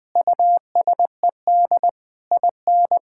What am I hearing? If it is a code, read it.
Morse code, heard as a keyed tone, about 20 wpm, 700 Hz. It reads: USED IN
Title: Sentences from Top 1000 Words - Review Part 6 - 20wpm